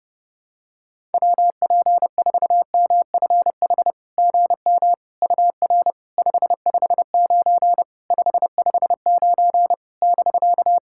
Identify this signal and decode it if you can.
Morse code, heard as a keyed tone, about 30 wpm, 700 Hz. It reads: WP4MFH GM UR 559 559 <BK>
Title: SOTA QSO Practice 30wpm